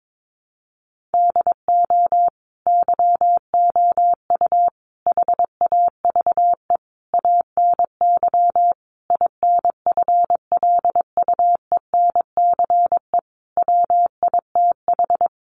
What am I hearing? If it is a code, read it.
Morse code, heard as a keyed tone, about 22 wpm, 700 Hz. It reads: DO YOU HAVE ANY INFLUENCE WITH
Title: Sentences from Top 1100 Words 22wpm